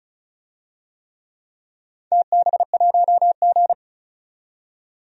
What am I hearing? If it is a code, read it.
Morse code, heard as a keyed tone, about 36 wpm, 700 Hz. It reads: TB1G
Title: DXCC-100 Call Signs - 4 Characters - 35wpm